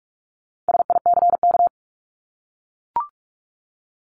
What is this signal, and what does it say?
Morse code, heard as a keyed tone, about 44 wpm, 700 Hz. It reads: SICK
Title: Top 1501-1750 Words 45wpm